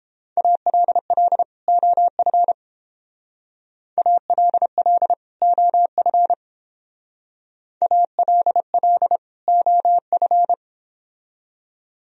Morse code, heard as a keyed tone, about 30 wpm, 700 Hz. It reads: ALL OF ALL OF ALL OF